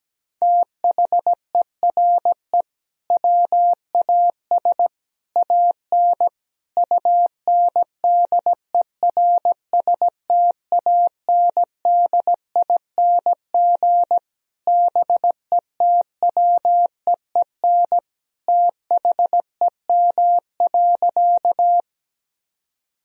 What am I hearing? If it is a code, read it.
Morse code, heard as a keyed tone, about 17 wpm, 700 Hz. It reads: THERE WAS AN UNDERSTANDING BETWEEN THEM.
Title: Sentences from Top 1000 Words - Review Part 3 - 17wpm